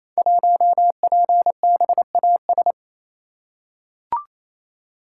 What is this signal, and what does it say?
Morse code, heard as a keyed tone, about 27 wpm, 700 Hz. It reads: 1PBAS
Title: Code Groups - 5 Letter:Numbers - 28wpm